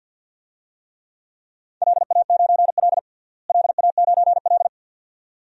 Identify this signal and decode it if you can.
Morse code, heard as a keyed tone, about 52 wpm, 700 Hz. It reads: LA9L LA9L
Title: DXCC-100 Call Signs - 4 Characters - Repeated 2x - 50wpm